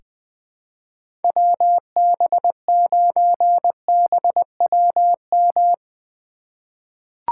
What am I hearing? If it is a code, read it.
Morse code, heard as a keyed tone, about 20 wpm, 700 Hz. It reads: WB9BWM
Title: USA Call Signs - 6 Characters - 2x Character Spacing - 20wpm